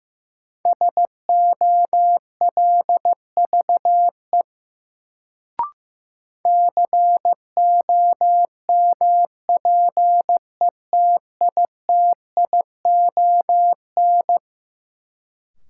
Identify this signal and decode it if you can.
Morse code, heard as a keyed tone, about 15 wpm, 700 Hz. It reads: SOLVE COMPETITION
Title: Sentences from Top 1000 Words - New Word Drill - 15wpm